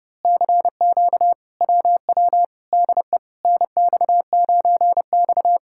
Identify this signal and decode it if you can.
Morse code, heard as a keyed tone, about 28 wpm, 700 Hz. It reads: CQ WW DE NX9X
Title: Speed-Racing Common QSO Phrases Encore - 30wpm